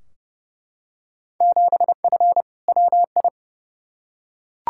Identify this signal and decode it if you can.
Morse code, heard as a keyed tone, about 31 wpm, 700 Hz. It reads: 7F WI
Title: Field Day Exchanges Encore - 30wpm